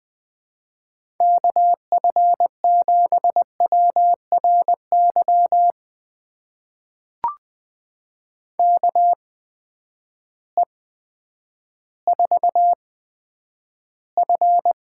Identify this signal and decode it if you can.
Morse code, heard as a keyed tone, about 20 wpm, spaced as for 10 wpm, 700 Hz. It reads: KF7WRY KE4F
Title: USA Call Signs - 6 Characters - 8x Character Spacing - 20wpm